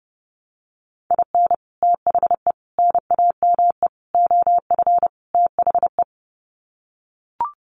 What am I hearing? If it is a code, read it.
Morse code, heard as a keyed tone, about 30 wpm, 700 Hz. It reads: IN THE NAME OF THE
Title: Sets of 5 Words Encore 30wpm